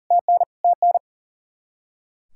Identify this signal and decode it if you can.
Morse code, heard as a keyed tone, about 40 wpm, 700 Hz. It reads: TN TN
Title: Practice QSO - Exchange 1 - Beginning - 40wpm